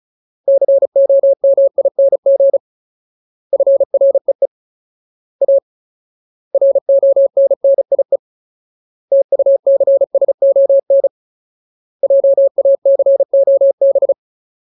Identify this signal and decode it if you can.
Morse code, heard as a keyed tone, about 34 wpm, 550 Hz. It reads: COMING FREE A RONNIE TUCSON JACOB
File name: Ultimate Word List / 35WPM Ultimate